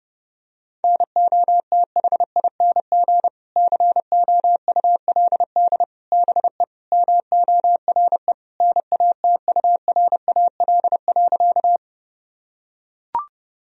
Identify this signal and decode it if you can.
Morse code, heard as a keyed tone, about 30 wpm, 700 Hz. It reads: NOTHING COULD BE MORE NATURAL.
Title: Sentences from Top 1000 Words - Review Part 4 - 30wpm